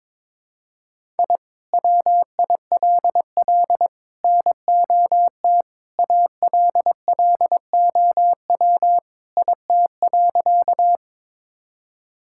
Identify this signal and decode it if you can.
Morse code, heard as a keyed tone, about 22 wpm, 700 Hz. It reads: I WILL NOT ALLOW IT.
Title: Sentences from Top 1000 Words - Review Part 5 - 22wpm